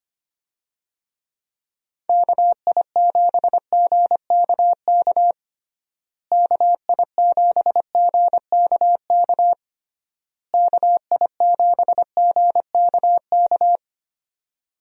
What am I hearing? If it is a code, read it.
Morse code, heard as a keyed tone, about 25 wpm, 700 Hz. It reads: KI7GKK KI7GKK KI7GKK
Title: USA Call Signs - 6 Characters - Repeated 3x - 25wpm